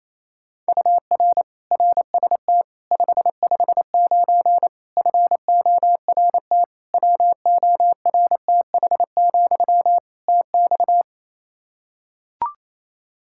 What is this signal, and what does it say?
Morse code, heard as a keyed tone, about 28 wpm, 700 Hz. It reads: UR RST 559 FORT WORTH, TX
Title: Practice QSO - Exchange 1 - Beginning - 28wpm